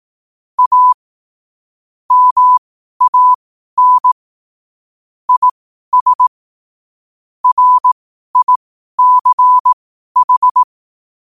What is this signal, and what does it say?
Morse code, heard as a keyed tone, about 18 wpm, spaced as for 12 wpm, 1000 Hz. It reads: A MAN IS RICH
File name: qotdfiles / 12wpm